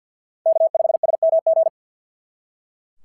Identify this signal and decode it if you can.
Morse code, heard as a keyed tone, about 44 wpm, 660 Hz. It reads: K5SMG
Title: SOTA QSO Practice 50wpm